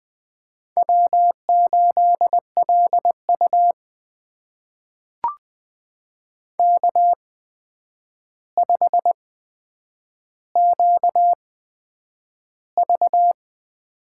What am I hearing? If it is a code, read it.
Morse code, heard as a keyed tone, about 20 wpm, spaced as for 9 wpm, 700 Hz. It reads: W8LU K5QV
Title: USA Call Signs - 4 Characters - 8x Character Spacing - 20wpm